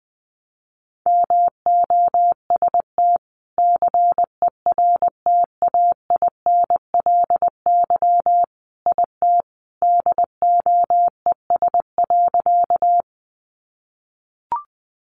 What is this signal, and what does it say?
Morse code, heard as a keyed tone, about 20 wpm, 700 Hz. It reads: MOST CERTAINLY IT DOES.
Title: Sentences from Top 900 Words 20wpm